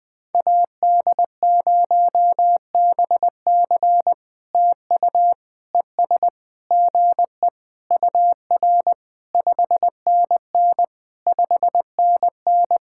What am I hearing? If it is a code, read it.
Morse code, heard as a keyed tone, about 20 wpm, 700 Hz. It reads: AD0BC TU ES GE UR 5NN 5NN
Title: POTA QSO Practice - 20wpm